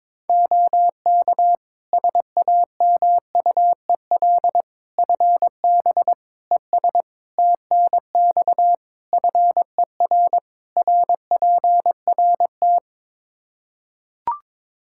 Morse code, heard as a keyed tone, about 22 wpm, 700 Hz. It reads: OK SAMUEL FB ES TNX FER RPRT
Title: K1USN SST QSO Practice 22wpm